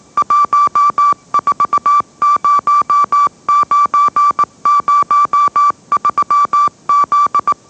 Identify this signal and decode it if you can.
Morse code, heard as a keyed tone, about 18 wpm, 1160 Hz. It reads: 140903Z
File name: mrx / mrx081 20wpm